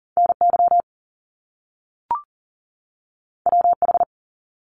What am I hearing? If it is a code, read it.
Morse code, heard as a keyed tone, about 39 wpm, 700 Hz. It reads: NY WH
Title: Top Two Letter Combinations in Words 40wpm